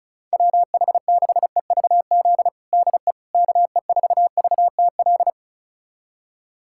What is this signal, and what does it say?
Morse code, heard as a keyed tone, about 36 wpm, 700 Hz. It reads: WH6EVZ DE KE4VTL
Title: USA Callsign DE Callsign - 6 Characters - 35wpm